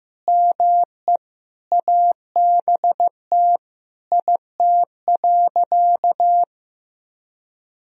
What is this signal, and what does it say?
Morse code, heard as a keyed tone, about 15 wpm, 700 Hz. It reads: ME ABT IT.
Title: Sentences from Top 200 Words and Abbreviations 15wpm